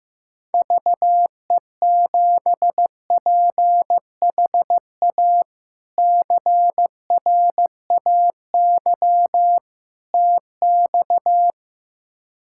Morse code, heard as a keyed tone, about 15 wpm, 700 Hz. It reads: VE7PHA CRAY TX
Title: K1USN SST QSO Practice - 15wpm